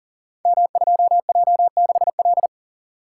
Morse code, heard as a keyed tone, about 42 wpm, 700 Hz. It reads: M2J6L
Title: Code Groups - 5 Letter:Numbers - 40wpm